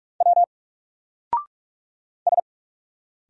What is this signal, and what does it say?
Morse code, heard as a keyed tone, about 44 wpm, 700 Hz. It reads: W S
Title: Single Character (Letter:Number:Punctuation:Prosign) Alt 45wpm